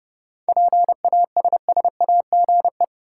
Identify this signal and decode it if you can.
Morse code, heard as a keyed tone, about 30 wpm, 700 Hz. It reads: PASSAGE
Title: Seven Letter Words - 30wpm